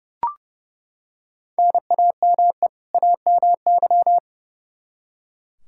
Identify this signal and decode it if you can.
Morse code, heard as a keyed tone, about 30 wpm, 700 Hz. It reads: NAME AMY
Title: Practice QSO - Exchange 1 - Beginning - 30wpm